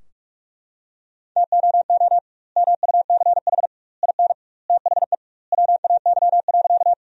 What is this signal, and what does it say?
Morse code, heard as a keyed tone, about 41 wpm, 700 Hz. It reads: TOO MUCH IN THE WAY.
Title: 2x Word-Spacing - Sentences from Top 200 Words - 45wpm